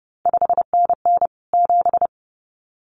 Morse code, heard as a keyed tone, about 30 wpm, 700 Hz. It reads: 5NN 7
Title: CQ WW DX Contest Exchange Encore - 30wpm